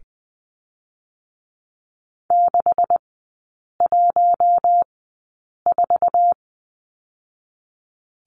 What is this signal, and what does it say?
Morse code, heard as a keyed tone, about 20 wpm, spaced as for 8 wpm, 700 Hz. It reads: 614
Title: 198 Lesson 32e - Letter k - Three Characters 20wpm